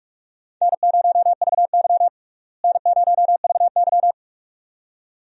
Morse code, heard as a keyed tone, about 46 wpm, 700 Hz. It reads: N0VY N0VY
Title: USA Call Signs - 4 Characters - Repeated 2x - 45wpm